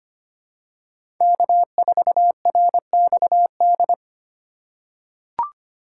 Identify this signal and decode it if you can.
Morse code, heard as a keyed tone, about 25 wpm, 700 Hz. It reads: K4RXD
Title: USA Call Signs - 4x Character Spacing - 25wpm